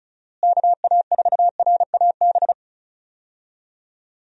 Morse code, heard as a keyed tone, about 36 wpm, 700 Hz. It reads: KA4RAB
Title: USA Call Signs - 6 Characters - Repeated 2x - 35wpm